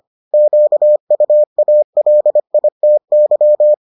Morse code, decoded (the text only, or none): QUALITY